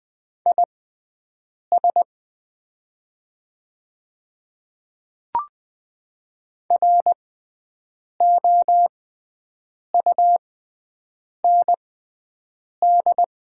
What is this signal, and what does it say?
Morse code, heard as a keyed tone, about 20 wpm, spaced as for 7 wpm, 700 Hz. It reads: IS ROUND
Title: 6x Character Spacing - Top 200 Words - 20wpm